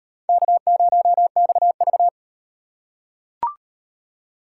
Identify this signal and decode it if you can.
Morse code, heard as a keyed tone, about 38 wpm, 700 Hz. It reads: K0XV